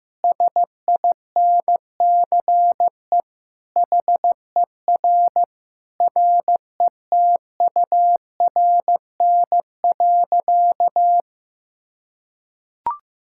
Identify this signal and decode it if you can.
Morse code, heard as a keyed tone, about 16 wpm, 700 Hz. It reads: SINCE HER RETURN.
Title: Sentences from Top 1000 Words - Review Part 6 - 15wpm